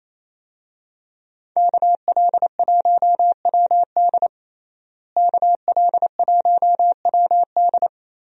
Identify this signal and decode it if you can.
Morse code, heard as a keyed tone, about 28 wpm, 700 Hz. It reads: KL1WD KL1WD
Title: USA Call Signs - 5 Characters - Repeated 2x - 28wpm